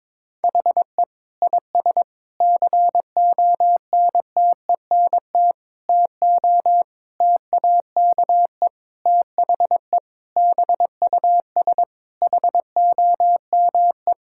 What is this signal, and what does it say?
Morse code, heard as a keyed tone, about 22 wpm, 700 Hz. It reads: HE IS CONTENT TO TAKE THE BUS HOME
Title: Sentences from Top 1000 Words - Review Part 6 - 22wpm